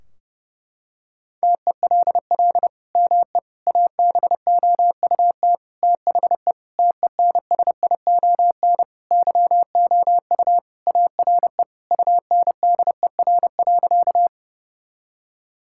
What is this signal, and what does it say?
Morse code, heard as a keyed tone, about 30 wpm, 700 Hz. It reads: TELL ME ABOUT THE TENSION YOU ARE UNDER.
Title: Sentences from Top 1600 Words 30wpm